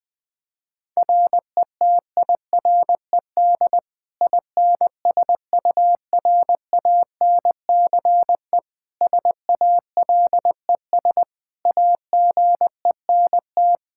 Morse code, heard as a keyed tone, about 20 wpm, 700 Hz. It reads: RETIRED INSURANCE SALES AGENT
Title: Practice QSO - Exchange 3 - More Info - 20wpm